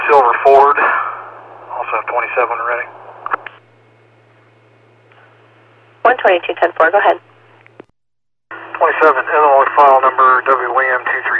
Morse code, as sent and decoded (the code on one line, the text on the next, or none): none